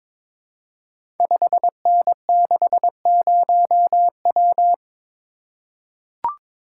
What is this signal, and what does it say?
Morse code, heard as a keyed tone, about 22 wpm, 700 Hz. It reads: 5N60W
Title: Code Groups - 5 Letter:Numbers - 22wpm